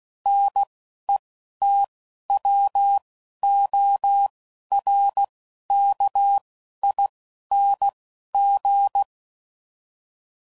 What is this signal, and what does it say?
Morse code, heard as a keyed tone, about 16 wpm, 800 Hz. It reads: N E T W O R K I N G
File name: Suffixes / Special INGs / Single INGs